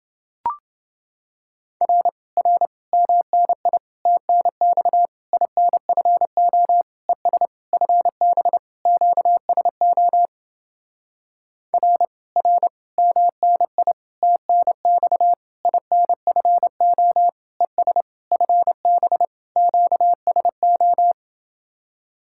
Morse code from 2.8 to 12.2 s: MNI TNX INFO ES FB QSO R

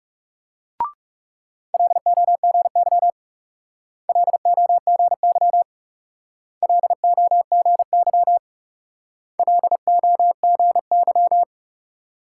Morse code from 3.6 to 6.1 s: LOGY

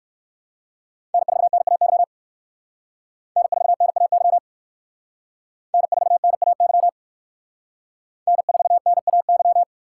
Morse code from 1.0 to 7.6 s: N4NAY N4NAY N4NAY

